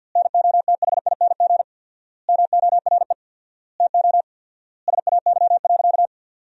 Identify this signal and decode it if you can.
Morse code, heard as a keyed tone, about 50 wpm, 700 Hz. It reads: NOTHING MORE TO SAY.
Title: 4x Word-Spacing - Sentences from Top 500 Words - 50wpm